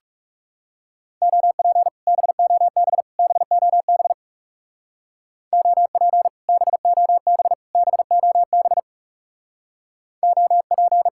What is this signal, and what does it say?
Morse code, heard as a keyed tone, about 38 wpm, 700 Hz. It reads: OP BOB BOB OP BOB BOB OP